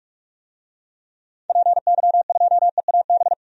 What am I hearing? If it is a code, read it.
Morse code, heard as a keyed tone, about 46 wpm, 700 Hz. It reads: PY2EAB
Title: DXCC-100 Call Signs - 6 Characters - 45wpm